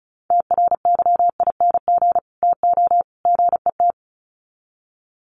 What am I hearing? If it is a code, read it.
Morse code, heard as a keyed tone, about 34 wpm, 700 Hz. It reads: TRYING TO GET
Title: Sets of 3 Words 35wpm